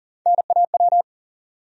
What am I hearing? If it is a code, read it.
Morse code, heard as a keyed tone, about 40 wpm, 700 Hz. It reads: NAW
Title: USA Call Sign Postfixes - 3 Letters 40wpm